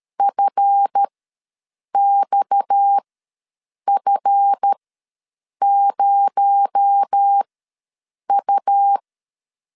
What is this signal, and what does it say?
Morse code, heard as a keyed tone, about 13 wpm, spaced as for 7 wpm, 800 Hz. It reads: FXF0U
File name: LEZIONI / 7 GF9Z / 7 GF ws 12wpm 3140